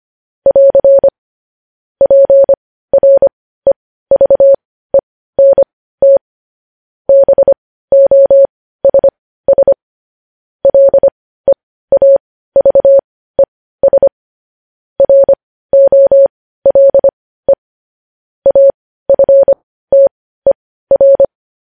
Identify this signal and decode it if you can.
Morse code, heard as a keyed tone, about 25 wpm, spaced as for 15 wpm, 550 Hz. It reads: <AR> PREVENT BOSS LEAVES ROLE AFTER
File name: podcast / News headlines-1x-25x15